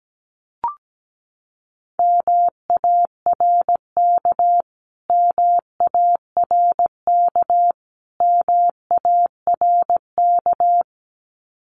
Morse code, heard as a keyed tone, about 17 wpm, 700 Hz. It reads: MARK MARK MARK
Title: Top 500 Words Repeated 17wpm